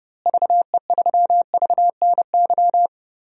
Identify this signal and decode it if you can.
Morse code, heard as a keyed tone, about 32 wpm, 700 Hz. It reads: VE3VNY
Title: DXCC-100 Call Signs - 6 Characters - Repeated 3x - 30wpm